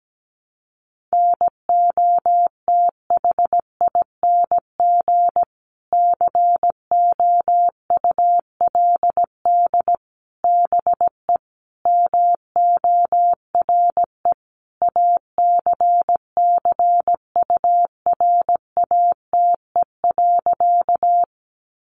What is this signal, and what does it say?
Morse code, heard as a keyed tone, about 17 wpm, 700 Hz. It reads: NOTHING COULD BE MORE ACCURATE.
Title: Sentences from Top 1200 Words 17wpm